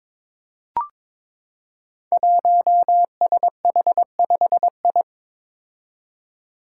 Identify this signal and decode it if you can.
Morse code, heard as a keyed tone, about 22 wpm, 700 Hz. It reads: 1SH5I
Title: Five Challenging Letter-Number Combinations 22wpm